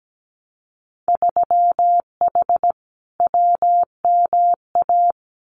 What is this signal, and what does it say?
Morse code, heard as a keyed tone, about 17 wpm, 700 Hz. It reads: 3H WMA
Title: Winter Field Day Exchanges Encore 17wpm